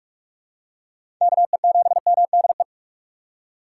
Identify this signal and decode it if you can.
Morse code, heard as a keyed tone, about 43 wpm, 700 Hz. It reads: KE7MDE
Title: USA Call Signs - 6 Characters - 3x Character Spacing - 45wpm